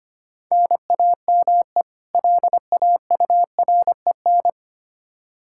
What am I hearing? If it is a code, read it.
Morse code, heard as a keyed tone, about 25 wpm, 700 Hz. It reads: NAME LAUREN
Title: K1USN SST QSO Practice 25wpm